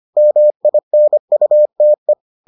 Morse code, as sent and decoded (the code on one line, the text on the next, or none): -- .. -. ..- - .
MINUTE